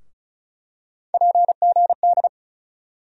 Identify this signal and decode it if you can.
Morse code, heard as a keyed tone, about 36 wpm, 700 Hz. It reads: PGD
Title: USA Call Sign Postfixes - 3 Letters 35wpm